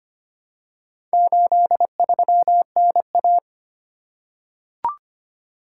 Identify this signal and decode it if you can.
Morse code, heard as a keyed tone, about 25 wpm, 700 Hz. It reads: 83NA
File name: practice / 25wpm / Four Letter-Number Combinations - 25wpm